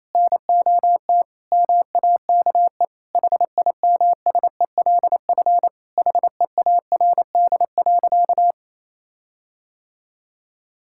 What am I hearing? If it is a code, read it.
Morse code, heard as a keyed tone, about 28 wpm, 700 Hz. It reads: NOT MAKE HIMSELF HEARD.
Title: Sentences from Top 1600 Words 28wpm